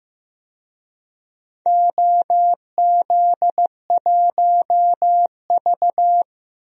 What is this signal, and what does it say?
Morse code, heard as a keyed tone, about 15 wpm, 700 Hz. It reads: OZ1V
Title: DXCC-100 Call Signs - 4 Characters - Repeated 2x - 15wpm